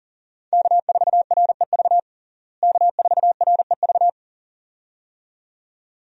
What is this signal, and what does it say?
Morse code, heard as a keyed tone, about 40 wpm, 700 Hz. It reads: K4REV K4REV
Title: USA Call Signs - 5 Characters - Repeated 2x - 40wpm